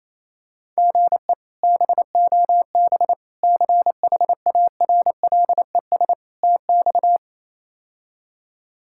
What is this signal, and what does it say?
Morse code, heard as a keyed tone, about 28 wpm, 700 Hz. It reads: GE BOB CHARLES TX
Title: K1USN SST QSO Practice - 28wpm